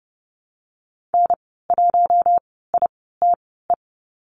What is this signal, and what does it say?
Morse code, heard as a keyed tone, about 30 wpm, spaced as for 17 wpm, 700 Hz. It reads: N1ITE
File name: Lesson 14g - Number 2 - Call Signs 30wpm